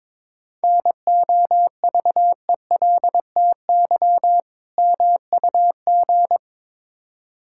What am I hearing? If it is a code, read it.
Morse code, heard as a keyed tone, about 22 wpm, 690 Hz. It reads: NOVELTY MUG
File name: Difficult Words with Hints - 22wpm